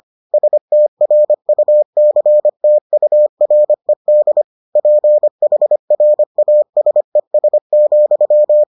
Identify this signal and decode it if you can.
Morse code, heard as a keyed tone, about 25 wpm, 600 Hz. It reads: STRUCTURED PHRASES,